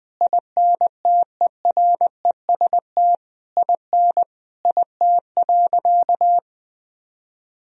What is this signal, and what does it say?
Morse code, heard as a keyed tone, about 20 wpm, 700 Hz. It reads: INTEREST IN IT.
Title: Sentences from Top 1000 Words - Review Part 3 - 20wpm